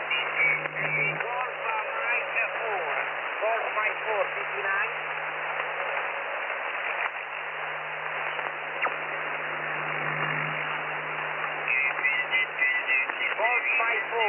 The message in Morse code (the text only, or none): none